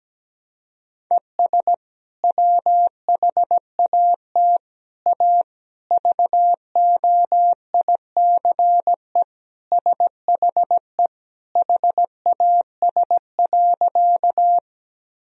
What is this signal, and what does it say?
Morse code, heard as a keyed tone, about 17 wpm, 700 Hz. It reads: ES WHAT A VOICE SHE HAS.
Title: Sentences from Top 500 Words and Abbreviations 17wpm